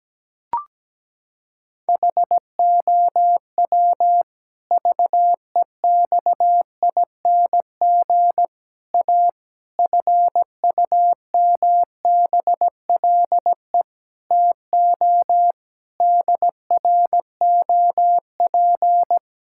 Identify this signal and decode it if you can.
Morse code, heard as a keyed tone, about 17 wpm, 700 Hz. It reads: HOW VEXING A FUMBLE TO DROP
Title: Pangram Challenge v2 - 17wpm